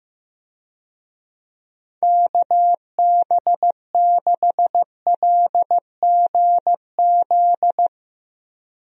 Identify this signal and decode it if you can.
Morse code, heard as a keyed tone, about 15 wpm, 700 Hz. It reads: KB6LGZ
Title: USA Call Signs - 6 Characters - Repeated 2x - 15wpm